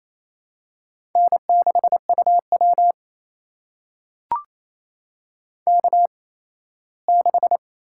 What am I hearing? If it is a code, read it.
Morse code, heard as a keyed tone, about 28 wpm, spaced as for 17 wpm, 700 Hz. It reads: N6UW K6
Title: USA Call Signs - 4 Characters - 8x Character Spacing - 28wpm